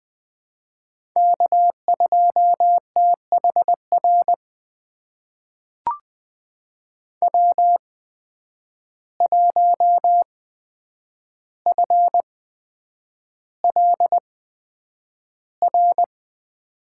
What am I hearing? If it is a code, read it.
Morse code, heard as a keyed tone, about 20 wpm, spaced as for 9 wpm, 700 Hz. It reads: K2THR W1FLR
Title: USA Call Signs - 5 Characters - 8x Character Spacing - 20wpm